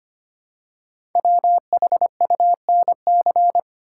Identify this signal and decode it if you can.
Morse code, heard as a keyed tone, about 25 wpm, 700 Hz. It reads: WHUNC
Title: Code Groups - 5 Letters - 25wpm